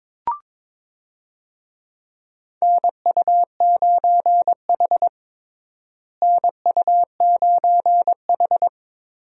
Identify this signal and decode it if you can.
Morse code, heard as a keyed tone, about 22 wpm, 700 Hz. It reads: NU9H NU9H
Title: USA Call Signs - 4 Characters - Repeated 3x - 22wpm